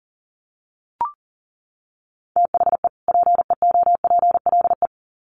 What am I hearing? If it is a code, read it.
Morse code, heard as a keyed tone, about 40 wpm, 700 Hz. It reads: THE PEOPLE
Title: Sets of 2 Words 40wpm